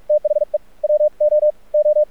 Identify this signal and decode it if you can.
Morse code, heard as a keyed tone, about 49 wpm, spaced as for 37 wpm, 600 Hz. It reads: THE WOO
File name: archived / CRSnet